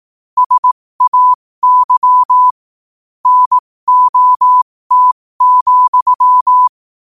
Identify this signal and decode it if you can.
Morse code, heard as a keyed tone, about 18 wpm, spaced as for 15 wpm, 1000 Hz. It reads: SAY NOT,
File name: qotdfiles / 15wpm